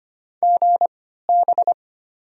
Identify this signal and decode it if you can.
Morse code, heard as a keyed tone, about 25 wpm, spaced as for 14 wpm, 700 Hz. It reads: GB
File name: Top 90 QSO Elements - 3x Character Spacing - 25wpm